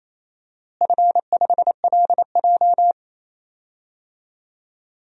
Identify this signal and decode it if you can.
Morse code, heard as a keyed tone, about 28 wpm, 700 Hz. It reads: F5LJ
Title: DXCC-100 Call Signs - 4 Characters - Repeated 2x - 28wpm